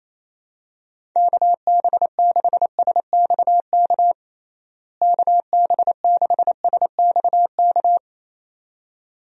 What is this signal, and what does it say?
Morse code, heard as a keyed tone, about 27 wpm, 700 Hz. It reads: KB6SXK KB6SXK
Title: USA Call Signs - 6 Characters - Repeated 2x - 28wpm